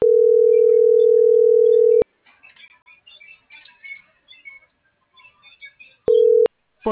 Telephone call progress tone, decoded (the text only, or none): ringback tone